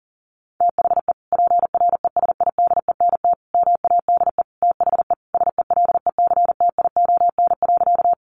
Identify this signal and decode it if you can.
Morse code, heard as a keyed tone, about 40 wpm, 700 Hz. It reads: THE PRESIDENT MADE THE SELECTION.
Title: Sentences from Top 1400 Words 40wpm